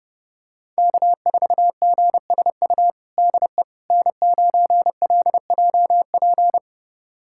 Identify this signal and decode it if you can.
Morse code, heard as a keyed tone, about 30 wpm, 700 Hz. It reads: K4GSU DE N9LJP
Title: USA Callsign DE Callsign 30wpm